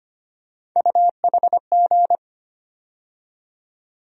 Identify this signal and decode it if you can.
Morse code, heard as a keyed tone, about 25 wpm, 700 Hz. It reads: UHG
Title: USA Call Sign Postfixes - 3 Letters 25wpm